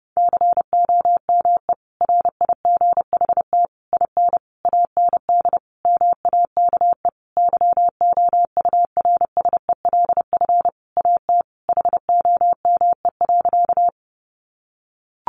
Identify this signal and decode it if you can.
Morse code, heard as a keyed tone, about 30 wpm, 700 Hz. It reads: COME RIGHT IN AND MAKE YOURSELF AT HOME.
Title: Sentences from Top 600 Words 30wpm